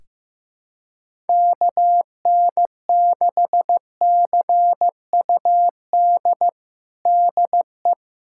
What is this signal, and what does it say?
Morse code, heard as a keyed tone, about 16 wpm, 700 Hz. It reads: KN6CUD DE
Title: USA Callsign DE Callsign - 6 Characters - 15wpm